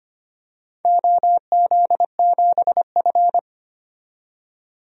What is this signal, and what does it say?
Morse code, heard as a keyed tone, about 25 wpm, 700 Hz. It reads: OZ7F
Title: DXCC-100 Call Signs - 4 Characters - 25wpm